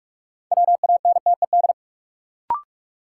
Morse code, heard as a keyed tone, about 44 wpm, 700 Hz. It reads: WANTED
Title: Sets of 2 Words - New Word Drill - 45wpm